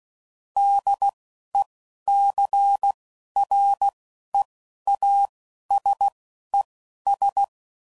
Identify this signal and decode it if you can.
Morse code, heard as a keyed tone, about 16 wpm, spaced as for 11 wpm, 800 Hz. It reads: DECREASES
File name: CW Code Groups @10WPM / Technical Subjects / XC